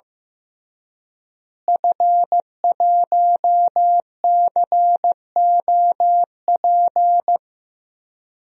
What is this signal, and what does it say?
Morse code, heard as a keyed tone, about 15 wpm, 700 Hz. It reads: F1COP
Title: DXCC-100 Call Signs - 5 Characters - Repeated 3x - 15wpm